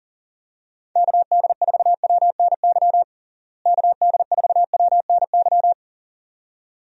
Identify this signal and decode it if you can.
Morse code, heard as a keyed tone, about 40 wpm, 700 Hz. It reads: KD4WNY KD4WNY
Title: USA Call Signs - 6 Characters - Repeated 2x - 40wpm